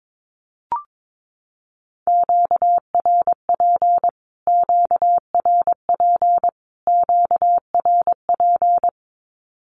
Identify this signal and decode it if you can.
Morse code, heard as a keyed tone, about 22 wpm, 700 Hz. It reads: QRP QRP QRP
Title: Q-Codes Repeated 3x 22wpm